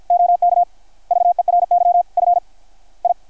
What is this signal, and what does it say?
Morse code, heard as a keyed tone, about 53 wpm, spaced as for 32 wpm, 690 Hz. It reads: OK VERY FI
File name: archived / CRSnet